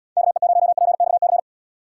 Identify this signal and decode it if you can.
Morse code, heard as a keyed tone, about 78 wpm, 700 Hz. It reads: KE0PXC